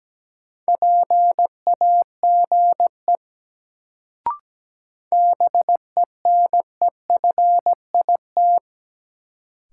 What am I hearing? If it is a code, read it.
Morse code, heard as a keyed tone, about 17 wpm, 700 Hz. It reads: PAGE BENEFIT
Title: Sets of 5 Words Encore - New Word Drill - 17wpm